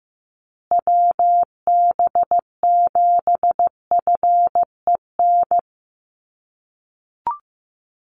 Morse code, heard as a keyed tone, about 15 wpm, 700 Hz. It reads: WB7FEN
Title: USA Call Signs Repeated 3x 15wpm